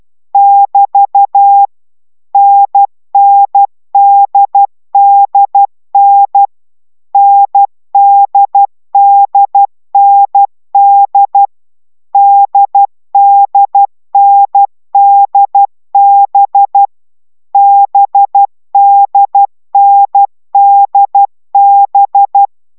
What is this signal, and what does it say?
Morse code, heard as a keyed tone, about 12 wpm, 800 Hz. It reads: <BT> NNDDN NDDND DDNDB BDNDB